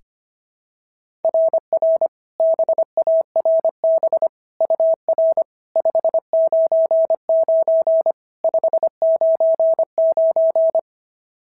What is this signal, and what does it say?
Morse code, heard as a keyed tone, about 25 wpm, 690 Hz. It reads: RR BARB UR 599 599
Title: SOTA QSO Practice 25wpm